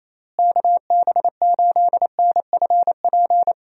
Morse code, heard as a keyed tone, about 28 wpm, 700 Hz. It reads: KB8NFP
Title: USA Call Signs - 6 Characters - Repeated 2x - 28wpm